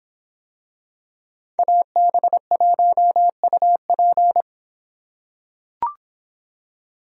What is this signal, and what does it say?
Morse code, heard as a keyed tone, about 24 wpm, 700 Hz. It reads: AB1UP